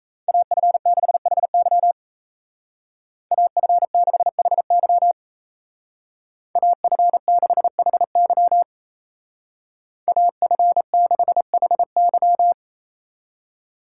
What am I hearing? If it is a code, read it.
Morse code, heard as a keyed tone, about 36 wpm, 700 Hz. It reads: AF6HY AF6HY AF6HY AF6HY